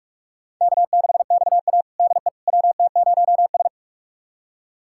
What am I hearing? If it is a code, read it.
Morse code, heard as a keyed tone, about 46 wpm, 700 Hz. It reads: K6XA DE WT0S
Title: USA Callsign DE Callsign - 4 Characters - 45wpm